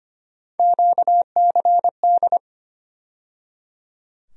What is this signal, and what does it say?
Morse code, heard as a keyed tone, about 25 wpm, 700 Hz. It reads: QCD
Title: USA Call Sign Postfixes - 3 Letters 25wpm